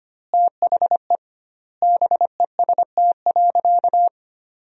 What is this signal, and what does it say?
Morse code, heard as a keyed tone, about 25 wpm, 700 Hz. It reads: THE BEST.
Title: 2x Word-Spacing - Sentences from Top 500 Words - 25wpm